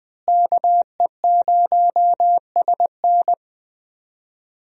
Morse code, heard as a keyed tone, about 21 wpm, 700 Hz. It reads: KE0SN
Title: USA Call Signs - 5 Characters - Repeated 3x - 20wpm